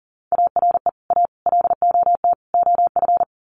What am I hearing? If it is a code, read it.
Morse code, heard as a keyed tone, about 40 wpm, 700 Hz. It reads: ARE A LOT OF
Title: Sets of 4 Words 40wpm